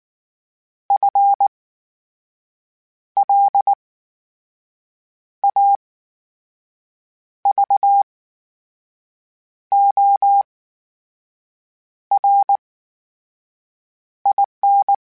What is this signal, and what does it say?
Morse code, heard as a keyed tone, about 19 wpm, spaced as for 5 wpm, 800 Hz. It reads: FLAVORIN
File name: Special INGs / Special INGs-mp3s / VerySlowToFaster Ing @18